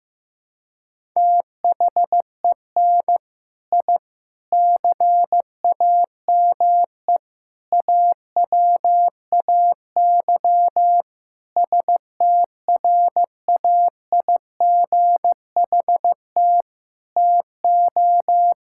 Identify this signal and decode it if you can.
Morse code, heard as a keyed tone, about 15 wpm, 700 Hz. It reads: THEN I CAME AWAY STRAIGHT TO
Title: Sentences from Top 1000 Words - Review Part 6 - 15wpm